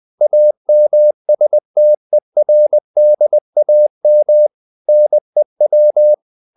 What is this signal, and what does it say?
Morse code, heard as a keyed tone, about 20 wpm, 600 Hz. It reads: AMSTERDAM NEW
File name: CW / Practice / NationalCapitals20